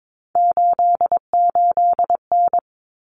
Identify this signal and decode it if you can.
Morse code, heard as a keyed tone, about 22 wpm, 700 Hz. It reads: 88N
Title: Common Cut Numbers - 3-Digit 22wpm